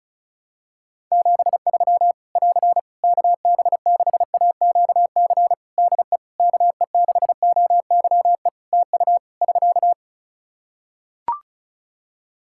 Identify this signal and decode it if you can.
Morse code, heard as a keyed tone, about 34 wpm, 700 Hz. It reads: 73 <AR> KB6AQC DE KE6OYE TU <SK>
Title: Practice QSO - Exchange 4 - Ending - 35wpm